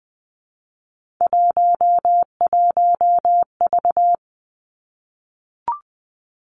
Morse code, heard as a keyed tone, about 20 wpm, 700 Hz. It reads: 11V
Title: Three Challenging Letter-Number Combinations 20wpm